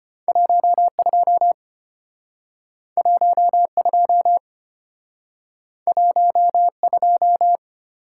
Morse code, heard as a keyed tone, about 34 wpm, 700 Hz. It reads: 12 12 12